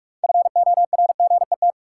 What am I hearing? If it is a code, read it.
Morse code, heard as a keyed tone, about 46 wpm, 700 Hz. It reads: FORGET
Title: Sentences from Top 1200 Words - New Word Drill - 45wpm